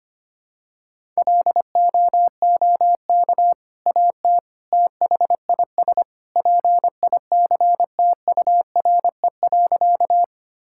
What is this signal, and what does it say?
Morse code, heard as a keyed tone, about 25 wpm, 700 Hz. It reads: LOOK AT THIS PICTURE.
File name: practice / 25wpm / Sentences with 2-Word Phrases - 25wpm